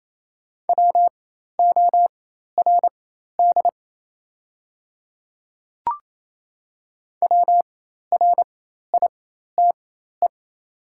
Morse code, heard as a keyed tone, about 28 wpm, spaced as for 13 wpm, 700 Hz. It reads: WORD WRITE
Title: 4x Character Spacing - Top 100 Words - 28wpm